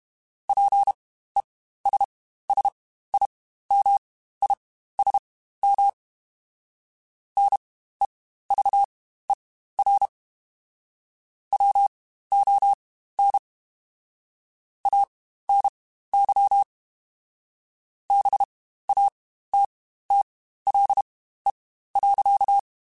Morse code, heard as a keyed tone, about 32 wpm, spaced as for 15 wpm, 800 Hz. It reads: PESSIMISM NEVER WON ANY BATTLE.
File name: Quotes Spaced / Quotes Spaced-mp3s / Pessimism sp @30